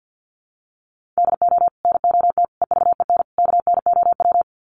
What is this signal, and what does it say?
Morse code, heard as a keyed tone, about 50 wpm, 700 Hz. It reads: DO NOT EVEN KNOW
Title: Sets of 4 Words 50wpm